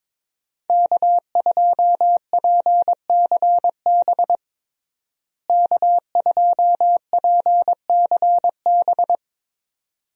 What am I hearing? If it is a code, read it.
Morse code, heard as a keyed tone, about 22 wpm, 700 Hz. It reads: K2PCB K2PCB
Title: USA Call Signs - 5 Characters - Repeated 2x - 22wpm